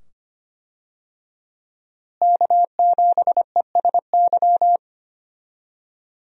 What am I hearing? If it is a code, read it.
Morse code, heard as a keyed tone, about 25 wpm, 700 Hz. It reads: K7ESY
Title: USA Call Signs - 5 Characters - Repeated 2x - 25wpm